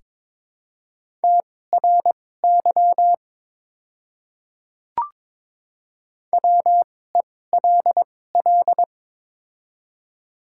Morse code, heard as a keyed tone, about 23 wpm, spaced as for 16 wpm, 700 Hz. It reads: TRY WELL
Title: 2x Character Spacing - Top 200 Words - 22wpm